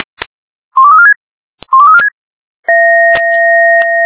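Five-tone selective call (ZVEI), heard as quick runs of five tones, tones of about 70 ms. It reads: 12346 12346
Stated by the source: ZVEI five-tone call 12346 received on SDR.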